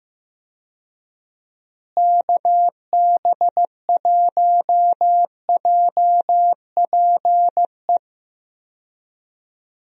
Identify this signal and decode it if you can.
Morse code, heard as a keyed tone, about 16 wpm, 700 Hz. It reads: KB1JPE
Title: USA Call Signs - 6 Characters - Repeated 3x - 15wpm